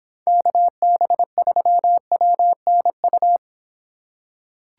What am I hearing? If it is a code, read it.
Morse code, heard as a keyed tone, about 24 wpm, 700 Hz. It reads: KB3WNU